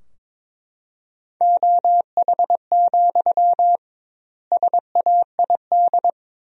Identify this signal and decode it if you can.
Morse code, heard as a keyed tone, about 22 wpm, 700 Hz. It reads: OH, SAID
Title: 2x Word-Spacing - Sentences from Top 500 Words - 22wpm